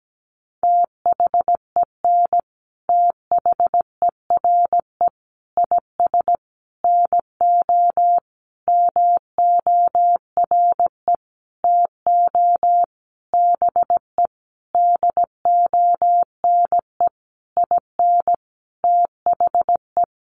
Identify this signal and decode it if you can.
Morse code, heard as a keyed tone, about 17 wpm, 700 Hz. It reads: THEN THERE IS NO MORE TO BE DONE IN THE
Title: Sentences from Top 700 Words 17wpm